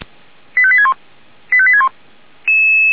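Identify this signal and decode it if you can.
Five-tone selective call (ZVEI), heard as quick runs of five tones, tones of about 70 ms. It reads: 85861 85861
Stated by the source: ZVEI five-tone selective call received with an SDR.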